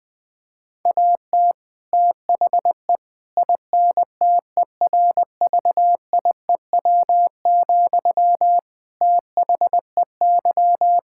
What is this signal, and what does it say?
Morse code, heard as a keyed tone, about 19 wpm, 700 Hz. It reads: AT THE INTERVIEW, THEY
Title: Sentences from Top 1600 Words 20wpm